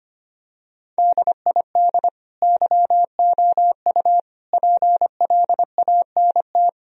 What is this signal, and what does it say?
Morse code, heard as a keyed tone, about 25 wpm, 700 Hz. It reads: DID YOU PLANT
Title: Sentences from Top 1000 Words - Review Part 1 - 25wpm